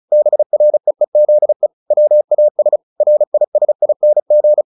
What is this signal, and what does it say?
Morse code, heard as a keyed tone, about 35 wpm, 600 Hz. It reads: BREEZE WAS RISING
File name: the old man and the sea / the old man and the sea 1 35wpm